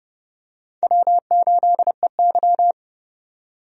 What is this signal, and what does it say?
Morse code, heard as a keyed tone, about 30 wpm, 700 Hz. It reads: W8EY
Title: USA Call Signs - 4 Characters - 30wpm